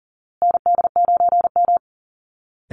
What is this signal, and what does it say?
Morse code, heard as a keyed tone, about 40 wpm, 700 Hz. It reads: ND9M
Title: Call Signs Repeated 2x - 40wpm